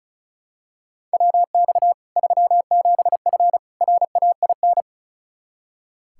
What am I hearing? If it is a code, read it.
Morse code, heard as a keyed tone, about 36 wpm, 700 Hz. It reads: WX 37F RAIN
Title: Practice QSO - Exchange 2 - Info - 35wpm